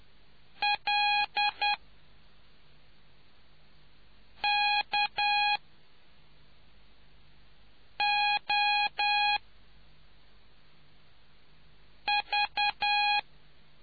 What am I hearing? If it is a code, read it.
Morse code, heard as a keyed tone, about 10 wpm, spaced as for 3 wpm, 830 Hz. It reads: LKOV